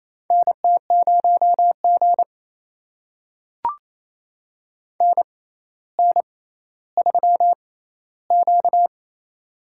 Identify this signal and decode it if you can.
Morse code, heard as a keyed tone, about 28 wpm, spaced as for 14 wpm, 700 Hz. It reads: NT0G NN3Q
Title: USA Call Signs - 4 Characters - 6x Character Spacing - 28wpm